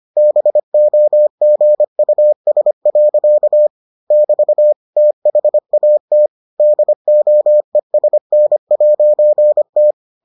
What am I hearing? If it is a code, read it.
Morse code, heard as a keyed tone, about 25 wpm, 600 Hz. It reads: BOGUS. <BT> THAT DOESN'T